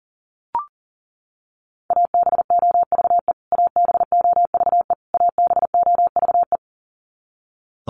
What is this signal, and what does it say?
Morse code, heard as a keyed tone, about 38 wpm, 700 Hz. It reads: ABOVE ABOVE ABOVE
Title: Top 500 Words Repeated 40wpm